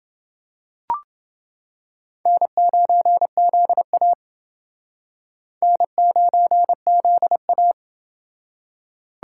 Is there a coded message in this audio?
Yes: Morse code from 2.3 to 7.7 s, heard as a keyed tone, about 29 wpm, 700 Hz. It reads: N9ZA N9ZA